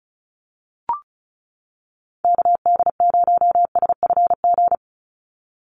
Morse code, heard as a keyed tone, about 34 wpm, 700 Hz. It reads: KD0SFG
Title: USA Call Signs 35wpm